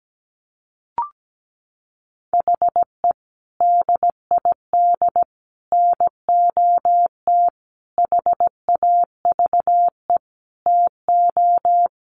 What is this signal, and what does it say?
Morse code, heard as a keyed tone, about 17 wpm, 700 Hz. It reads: HE DID NOT HAVE TO
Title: Sets of 5 Words 17wpm